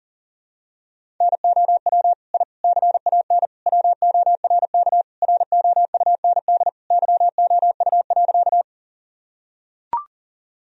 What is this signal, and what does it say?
Morse code, heard as a keyed tone, about 40 wpm, 700 Hz. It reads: NOW I CAN WORK ROUND YOU.
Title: Sentences from Top 1000 Words - Review Part 1 - 40wpm